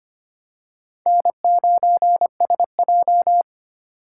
Morse code, heard as a keyed tone, about 25 wpm, 700 Hz. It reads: N9SJ
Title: USA Call Signs - 4 Characters - 3x Character Spacing - 25wpm